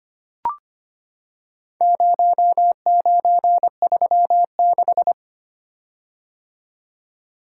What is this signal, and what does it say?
Morse code, heard as a keyed tone, about 25 wpm, 700 Hz. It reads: 0936
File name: practice / 25wpm / Four Letter-Number Combinations - 25wpm